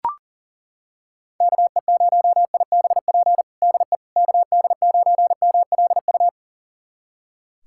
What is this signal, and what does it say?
Morse code, heard as a keyed tone, about 40 wpm, 700 Hz. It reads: KE0IBP DE KD9MLU
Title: USA Callsign DE Callsign - 6 Characters - 40wpm